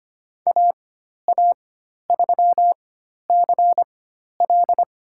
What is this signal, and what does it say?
Morse code, heard as a keyed tone, about 25 wpm, spaced as for 12 wpm, 700 Hz. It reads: AA3CL
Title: USA Call Signs - 5 Characters - 4x Character Spacing - 25wpm